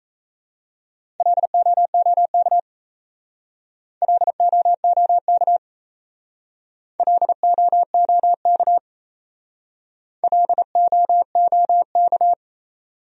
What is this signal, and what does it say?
Morse code, heard as a keyed tone, about 35 wpm, 700 Hz. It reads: LOOK LOOK LOOK LOOK
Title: Speed-Racing Top 100 Words 28wpm